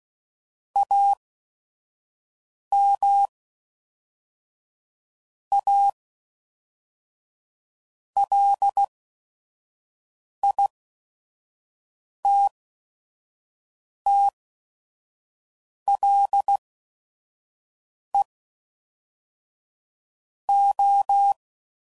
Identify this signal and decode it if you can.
Morse code, heard as a keyed tone, about 16 wpm, spaced as for 5 wpm, 800 Hz. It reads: AM A LITTLE O
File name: CW Code Groups @5WPM / QSOs / kc7sac